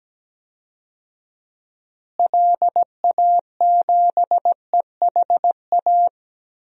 Morse code, heard as a keyed tone, about 17 wpm, 700 Hz. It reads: LA7EHA
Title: DXCC-100 Call Signs - 6 Characters - Repeated 3x - 17wpm